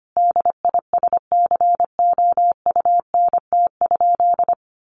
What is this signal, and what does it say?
Morse code, heard as a keyed tone, about 25 wpm, 700 Hz. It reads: DISCOUNT?
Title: Sentences from Top 1200 Words 25wpm